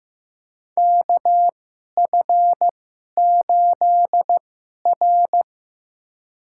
Morse code, heard as a keyed tone, about 15 wpm, spaced as for 11 wpm, 700 Hz. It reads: KF8R
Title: USA Call Signs - 4 Characters - 2x Character Spacing - 15wpm